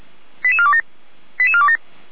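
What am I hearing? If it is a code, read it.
Five-tone selective call (ZVEI), heard as quick runs of five tones, tones of about 70 ms. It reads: 88427 88427
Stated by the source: ZVEI five-tone call received by SDR, from the feuerwehr-orlamuende.org (f85360) audio archive.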